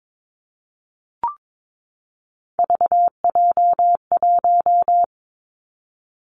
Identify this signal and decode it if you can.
Morse code, heard as a keyed tone, about 22 wpm, 700 Hz. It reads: VJ1
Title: Three Challenging Letter-Number Combinations 22wpm